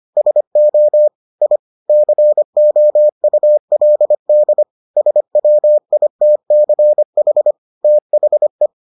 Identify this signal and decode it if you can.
Morse code, heard as a keyed tone, about 25 wpm, 600 Hz. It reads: SO I COULD SWITCH THE